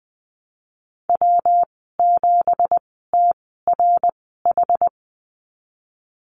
Morse code, heard as a keyed tone, about 20 wpm, spaced as for 15 wpm, 700 Hz. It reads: W7TRH
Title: 179 Lesson 29g - Slash - Call Signs 20wpm